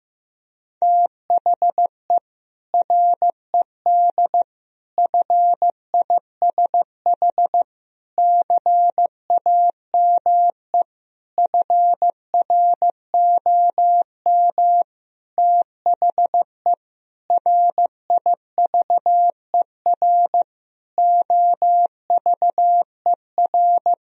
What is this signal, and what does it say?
Morse code, heard as a keyed tone, about 15 wpm, 700 Hz. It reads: THE RED FISH CAME FROM THE RIVER OVER
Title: Sentences from Top 1000 Words - Review Part 2 - 15wpm